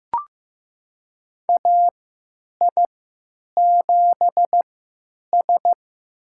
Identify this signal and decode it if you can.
Morse code, heard as a keyed tone, about 15 wpm, spaced as for 9 wpm, 700 Hz. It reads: AI7S
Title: USA Call Signs - 4 Characters - 3x Character Spacing - 15wpm